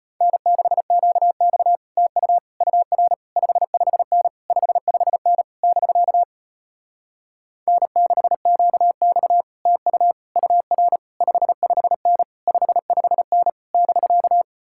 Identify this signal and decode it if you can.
Morse code, heard as a keyed tone, about 33 wpm, 700 Hz. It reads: N6QX TU UR 55N 55N <BK> N6QX TU UR 55N 55N <BK>